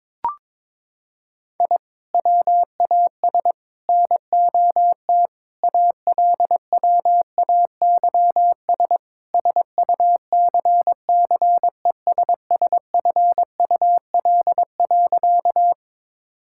Morse code, heard as a keyed tone, about 22 wpm, 700 Hz. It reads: I WAS NOT ALWAYS SUCCESSFUL.
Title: Sentences from Top 1000 Words - Review Part 6 - 22wpm